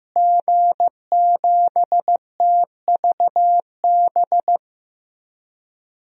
Morse code, heard as a keyed tone, about 15 wpm, 700 Hz. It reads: G7TVB
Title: DXCC-100 Call Signs - 5 Characters - Repeated 3x - 15wpm